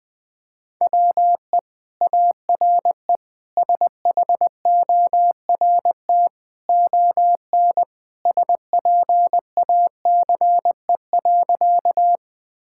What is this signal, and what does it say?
Morse code, heard as a keyed tone, about 20 wpm, 700 Hz. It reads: WE ARE SHORT ON SPACE.
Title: Sentences from Top 1000 Words - Review Part 5 - 20wpm